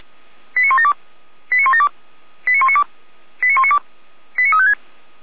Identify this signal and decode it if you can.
Five-tone selective call (ZVEI), heard as quick runs of five tones, tones of about 70 ms. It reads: 89172 89172 89192 89192 89356